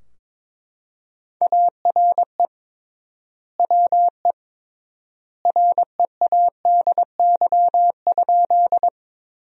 Morse code, heard as a keyed tone, about 22 wpm, 700 Hz. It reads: ARE WE READY?
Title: 3x Word-Spacing - Sentences from Top 400 Words - 22wpm